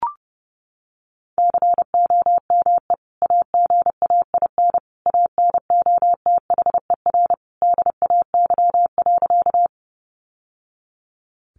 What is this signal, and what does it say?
Morse code, heard as a keyed tone, about 30 wpm, 700 Hz. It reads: COME AGAIN ANOTHER DAY.
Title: Sentences from Top 600 Words 30wpm